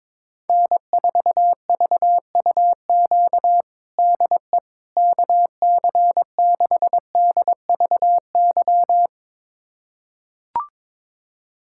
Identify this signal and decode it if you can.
Morse code, heard as a keyed tone, about 22 wpm, 700 Hz. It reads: N4VUQ DE KC6DVY
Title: USA Callsign DE Callsign 22wpm